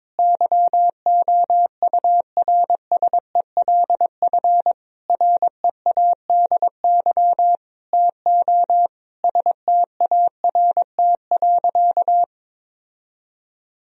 Morse code, heard as a keyed tone, about 21 wpm, 700 Hz. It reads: YOURSELF READY TO START.
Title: Sentences from Top 1000 Words - Review Part 1 - 22wpm